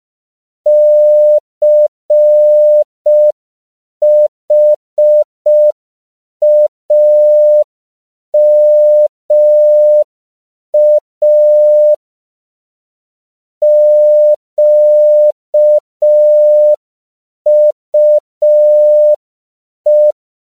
Morse code, heard as a keyed tone, about 5 wpm, 600 Hz. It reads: CHAMA QUE
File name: CW / 5PPM